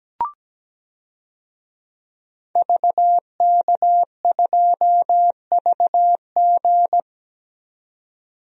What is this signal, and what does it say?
Morse code, heard as a keyed tone, about 17 wpm, 700 Hz. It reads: VK2VG
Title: DXCC-100 Call Signs - 5 Characters - 17wpm